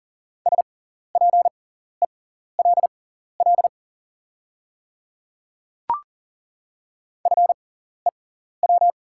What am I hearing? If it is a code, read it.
Morse code, heard as a keyed tone, about 39 wpm, spaced as for 14 wpm, 690 Hz. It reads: SPELL FEW
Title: 6x Character Spacing - Top 300 Words - 40wpm